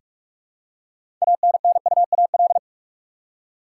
Morse code, heard as a keyed tone, about 46 wpm, 700 Hz. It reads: ANNUAL
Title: Six Letter Words - 45wpm